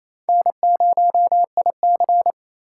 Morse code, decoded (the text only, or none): N0IC